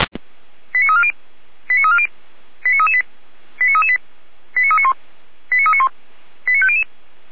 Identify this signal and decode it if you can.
Five-tone selective call (ZVEI), heard as quick runs of five tones, tones of about 70 ms. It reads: 89340 89340 89338 89338 89391 89391 89500